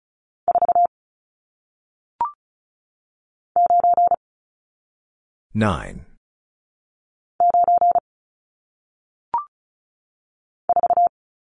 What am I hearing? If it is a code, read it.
Morse code, heard as a keyed tone, about 36 wpm, 700 Hz. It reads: 4 9 9 4
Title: Numbers - 1-Digit 35wpm